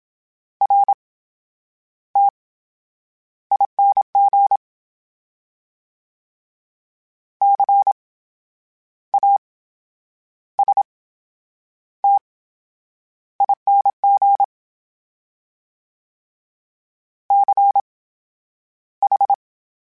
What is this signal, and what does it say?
Morse code, heard as a keyed tone, about 27 wpm, spaced as for 9 wpm, 800 Hz. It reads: RTING CASTING CH
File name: Special INGs / Special INGs-mp3s / VerySlowToFaster Ing @25